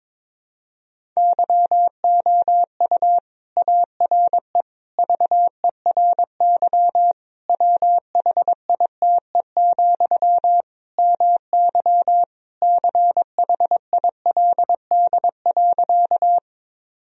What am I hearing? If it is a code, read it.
Morse code, heard as a keyed tone, about 21 wpm, 700 Hz. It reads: YOU ARE VERY WHITE, MY CHILD.
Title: Sentences from Top 1000 Words - Review Part 1 - 22wpm